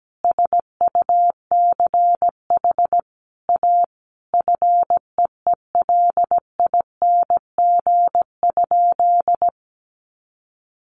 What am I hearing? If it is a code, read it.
Morse code, heard as a keyed tone, about 17 wpm, 700 Hz. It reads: SUCH A FEELING?
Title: Sentences from Top 1200 Words 17wpm